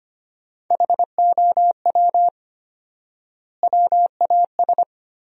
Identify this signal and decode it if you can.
Morse code, heard as a keyed tone, about 25 wpm, 700 Hz. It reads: HOW WAS
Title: 4x Word-Spacing - Sentences from Top 100 Words - 25wpm